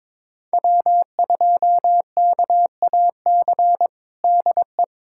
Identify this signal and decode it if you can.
Morse code, heard as a keyed tone, about 22 wpm, 700 Hz. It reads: W2KAC DE
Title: USA Callsign DE Callsign - 5 Characters - 22wpm